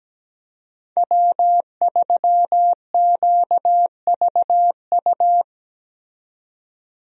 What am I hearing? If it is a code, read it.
Morse code, heard as a keyed tone, about 17 wpm, 700 Hz. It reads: W3QVU
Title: USA Call Signs - 5 Characters - 17wpm